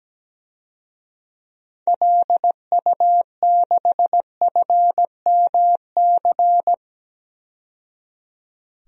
Morse code, heard as a keyed tone, about 17 wpm, 700 Hz. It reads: LU6FMC
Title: DXCC-100 Call Signs - 6 Characters - 17wpm